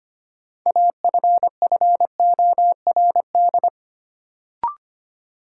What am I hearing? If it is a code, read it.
Morse code, heard as a keyed tone, about 25 wpm, 700 Hz. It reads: AFFORD
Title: Sentences from Top 1100 Words - New Word Drill Alt - 25wpm